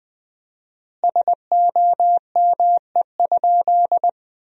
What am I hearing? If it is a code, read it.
Morse code, heard as a keyed tone, about 19 wpm, 700 Hz. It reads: SOME?
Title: 4x Word-Spacing - Sentences from Top 100 Words - 20wpm